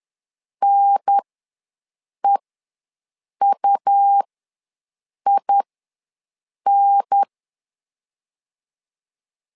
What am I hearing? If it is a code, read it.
Morse code, heard as a keyed tone, about 10 wpm, spaced as for 6 wpm, 800 Hz. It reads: NEUIN
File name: LEZIONI / 3 MWRO / 3 M ws 10wpm 2121